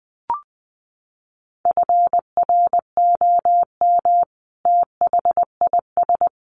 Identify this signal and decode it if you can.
Morse code, heard as a keyed tone, about 20 wpm, 700 Hz. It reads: FROM THIS
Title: Sentences from Top 1500 Words 20wpm